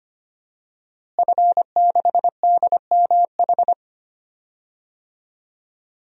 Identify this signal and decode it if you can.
Morse code, heard as a keyed tone, about 25 wpm, 700 Hz. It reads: F6DMH
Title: DXCC-100 Call Signs - 5 Characters - Repeated 3x - 25wpm